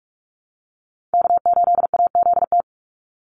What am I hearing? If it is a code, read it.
Morse code, heard as a keyed tone, about 44 wpm, 700 Hz. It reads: K8AZT
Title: Call Signs Repeated 2x - 45wpm